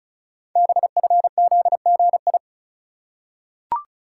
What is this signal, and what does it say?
Morse code, heard as a keyed tone, about 36 wpm, 700 Hz. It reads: BFZGI
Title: Code Groups - 5 Letters - 35wpm